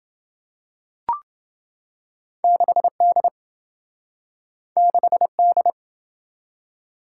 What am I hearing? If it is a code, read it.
Morse code, heard as a keyed tone, about 27 wpm, 700 Hz. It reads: 6D 6D